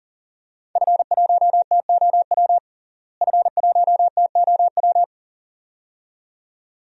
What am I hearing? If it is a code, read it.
Morse code, heard as a keyed tone, about 40 wpm, 700 Hz. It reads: F1TOW F1TOW
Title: DXCC-100 Call Signs - 5 Characters - Repeated 2x - 40wpm